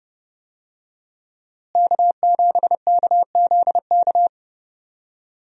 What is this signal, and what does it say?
Morse code, heard as a keyed tone, about 30 wpm, 700 Hz. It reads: K7KZK
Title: USA Call Signs - 5 Characters - Repeated 2x - 30wpm